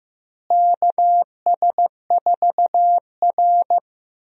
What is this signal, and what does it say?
Morse code, heard as a keyed tone, about 15 wpm, 700 Hz. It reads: KS4R
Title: USA Call Signs - 4 Characters - Repeated 3x - 15wpm